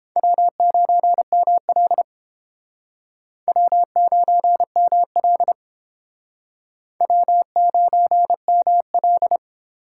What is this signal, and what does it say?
Morse code, heard as a keyed tone, about 32 wpm, 700 Hz. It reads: W9ML W9ML W9ML